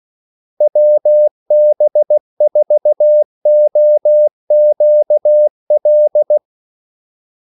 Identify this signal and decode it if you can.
Morse code, heard as a keyed tone, about 16 wpm, 600 Hz. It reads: WB4OQL
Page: Kaldesignaler 16 wpm